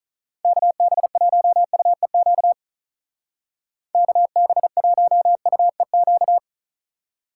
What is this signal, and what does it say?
Morse code, heard as a keyed tone, about 44 wpm, 700 Hz. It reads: KB1UEQ KB1UEQ
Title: Speed Racing - USA 2x3 Call Signs 35wpm